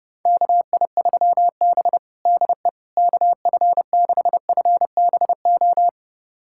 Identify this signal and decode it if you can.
Morse code, heard as a keyed tone, about 30 wpm, 700 Hz. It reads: KI3B DE KF6FBO
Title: USA Callsign DE Callsign 30wpm